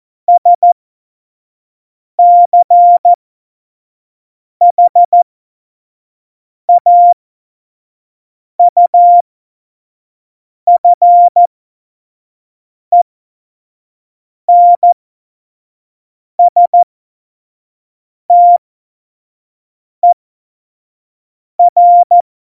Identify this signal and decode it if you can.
Morse code, heard as a keyed tone, about 14 wpm, spaced as for 5 wpm, 700 Hz. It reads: SCHAUFENSTER
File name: Morsekurs / Witze / Witz 08. Mai 2018, 25 F70